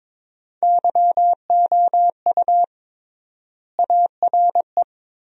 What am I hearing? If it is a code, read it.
Morse code, heard as a keyed tone, about 21 wpm, 700 Hz. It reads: YOU ARE
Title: 3x Word-Spacing - Sentences from Top 400 Words - 22wpm